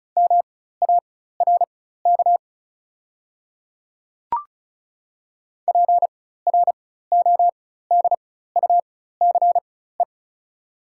Morse code, heard as a keyed tone, about 36 wpm, spaced as for 16 wpm, 700 Hz. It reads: MARK PRODUCE
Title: 4x Character Spacing - Top 500 Words - 35wpm